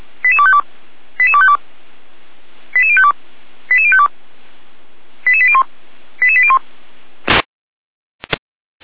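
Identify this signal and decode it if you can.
Five-tone selective call (ZVEI), heard as quick runs of five tones, tones of about 70 ms. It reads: 88252 88252 88052 88052 88801 88801
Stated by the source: ZVEI five-tone selective call 88191 received via SDR.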